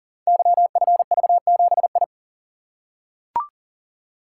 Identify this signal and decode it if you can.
Morse code, heard as a keyed tone, about 42 wpm, 690 Hz. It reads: YFV7I
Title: Code Groups - 5 Letter:Numbers - 40wpm